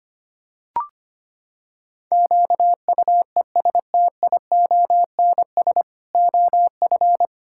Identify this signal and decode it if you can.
Morse code, heard as a keyed tone, about 25 wpm, 700 Hz. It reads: QUESTIONS OF
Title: Pangram Challenge v2 - 25wpm